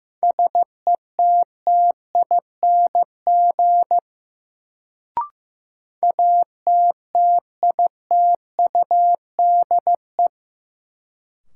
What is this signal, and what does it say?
Morse code, heard as a keyed tone, about 15 wpm, 700 Hz. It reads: SETTING ATTITUDE
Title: Sentences from Top 1300 Words - New Word Drill - 15wpm